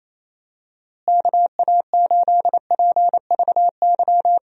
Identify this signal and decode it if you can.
Morse code, heard as a keyed tone, about 28 wpm, 700 Hz. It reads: KA8PVY
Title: USA Call Signs - 6 Characters - Repeated 3x - 28wpm